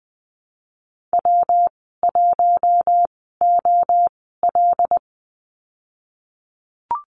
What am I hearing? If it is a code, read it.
Morse code, heard as a keyed tone, about 20 wpm, spaced as for 15 wpm, 700 Hz. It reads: W1OL
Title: 173 Lesson 28g - Number 9 - Call Signs 20wpm